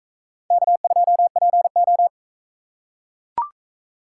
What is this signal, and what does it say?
Morse code, heard as a keyed tone, about 44 wpm, 700 Hz. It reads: K2PO